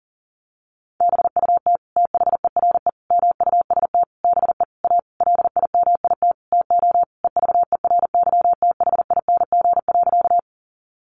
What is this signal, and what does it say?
Morse code, heard as a keyed tone, about 37 wpm, 700 Hz. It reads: BUT THERE MUST BE A LIMIT TO EVERYTHING.
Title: Sentences from Top 1500 Words 40wpm